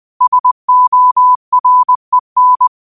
Morse code, heard as a keyed tone, about 20 wpm, 1000 Hz. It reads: SOREN
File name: qotdfiles / 20wpm